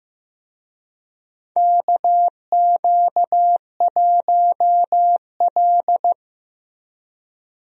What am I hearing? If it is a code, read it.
Morse code, heard as a keyed tone, about 15 wpm, 700 Hz. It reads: KQ1L
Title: USA Call Signs - 4 Characters - Repeated 3x - 15wpm